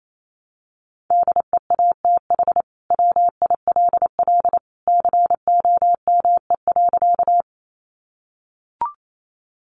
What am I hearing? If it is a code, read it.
Morse code, heard as a keyed tone, about 28 wpm, 700 Hz. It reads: DEATH WILL COME.
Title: Sentences from Top 1100 Words 28wpm